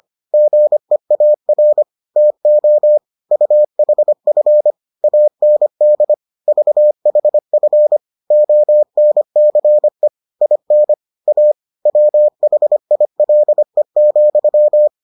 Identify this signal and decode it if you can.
Morse code, heard as a keyed tone, about 25 wpm, 600 Hz. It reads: GEAR TO UHF AND VHF ONCE IN A WHILE,